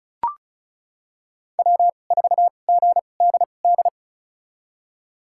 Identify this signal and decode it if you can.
Morse code, heard as a keyed tone, about 36 wpm, spaced as for 25 wpm, 700 Hz. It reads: W4GDD
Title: USA Call Signs - 5 Characters - 2x Character Spacing - 35wpm